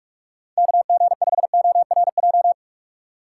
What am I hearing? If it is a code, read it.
Morse code, heard as a keyed tone, about 47 wpm, 700 Hz. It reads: KG5ORJ
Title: USA Call Signs - 3x Character Spacing - 45wpm